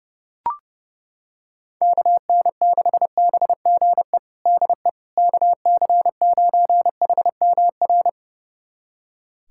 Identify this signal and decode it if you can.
Morse code, heard as a keyed tone, about 30 wpm, 700 Hz. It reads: KN6BGE DE KC9HMR
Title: USA Callsign DE Callsign 30wpm